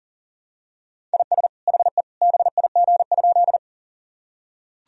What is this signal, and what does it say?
Morse code, heard as a keyed tone, about 40 wpm, 700 Hz. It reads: IS HE BIG?
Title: Sentences from Top 1000 Words - Review Part 1 - 40wpm